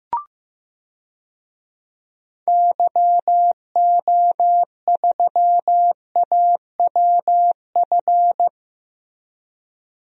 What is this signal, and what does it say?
Morse code, heard as a keyed tone, about 15 wpm, 700 Hz. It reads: YO3AWF
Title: DXCC-100 Call Signs - 6 Characters - 15wpm